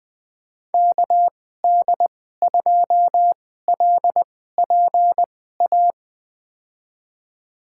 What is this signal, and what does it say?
Morse code, heard as a keyed tone, about 20 wpm, spaced as for 14 wpm, 700 Hz. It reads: KD2LPA
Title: USA Call Signs - 6 Characters - 2x Character Spacing - 20wpm